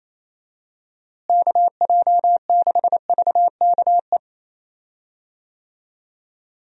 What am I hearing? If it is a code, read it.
Morse code, heard as a keyed tone, about 28 wpm, 700 Hz. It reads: KJ6VKE
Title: USA Call Signs - 6 Characters - Repeated 3x - 28wpm